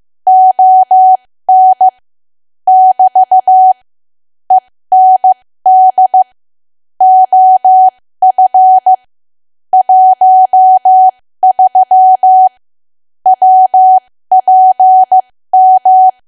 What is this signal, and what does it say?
Morse code, heard as a keyed tone, about 15 wpm, spaced as for 13 wpm, 750 Hz. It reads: ON <BT> END OF 13 WPM
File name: cwa / W1AW-Level2 / 13WPM